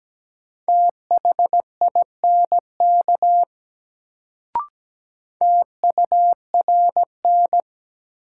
Morse code, heard as a keyed tone, about 17 wpm, 700 Hz. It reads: THINK TURN
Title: Sentences from Top 200 Words - New Word Drill - 17wpm